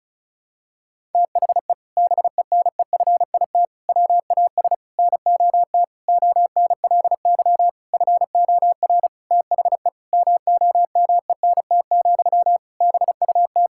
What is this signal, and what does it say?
Morse code, heard as a keyed tone, about 36 wpm, 700 Hz. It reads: THE BENEFIT WAS NOT ONLY FOR THE MOMENT, BUT
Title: Sentences from Top 1000 Words - Review Part 6 - 35wpm